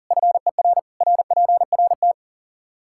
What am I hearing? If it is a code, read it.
Morse code, heard as a keyed tone, about 40 wpm, 700 Hz. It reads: FER RPRT
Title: Practice QSO - Exchange 2 - Info - 40wpm